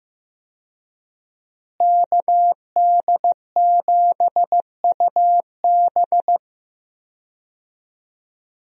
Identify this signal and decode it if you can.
Morse code, heard as a keyed tone, about 15 wpm, 700 Hz. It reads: KD7UB
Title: USA Call Signs - 5 Characters - Repeated 2x - 15wpm